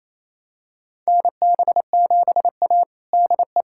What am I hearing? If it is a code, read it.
Morse code, heard as a keyed tone, about 28 wpm, 700 Hz. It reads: NB7A DE
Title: USA Callsign DE Callsign - 4 Characters - 28wpm